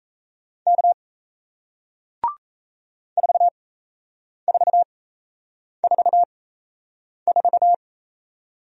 Morse code, heard as a keyed tone, about 37 wpm, spaced as for 8 wpm, 700 Hz. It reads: K 4444